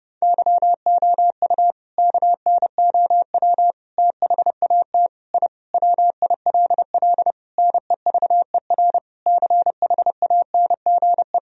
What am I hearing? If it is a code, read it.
Morse code, heard as a keyed tone, about 30 wpm, 700 Hz. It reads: YOU KNOW THAT I WILL NEVER CHANGE
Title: Sentences from Top 1000 Words - Review Part 1 - 30wpm